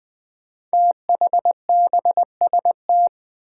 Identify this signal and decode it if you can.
Morse code, heard as a keyed tone, about 20 wpm, 700 Hz. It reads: THBST
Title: Code Groups - 5 Letters - 20wpm